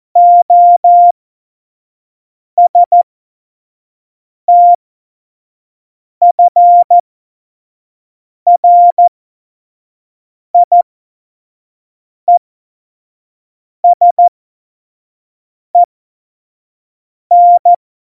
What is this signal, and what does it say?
Morse code, heard as a keyed tone, about 14 wpm, spaced as for 5 wpm, 700 Hz. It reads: OSTFRIESEN
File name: Morsekurs / Witze / Witz 20. Oktober 2020, 25 F70